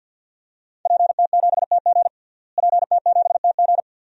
Morse code, heard as a keyed tone, about 47 wpm, 700 Hz. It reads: PT7TG PT7TG
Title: DXCC-100 Call Signs - 5 Characters - Repeated 2x - 50wpm